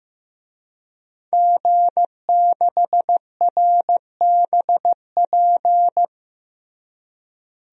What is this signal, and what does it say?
Morse code, heard as a keyed tone, about 15 wpm, 700 Hz. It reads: G6RBP
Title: DXCC-100 Call Signs - 5 Characters - Repeated 2x - 15wpm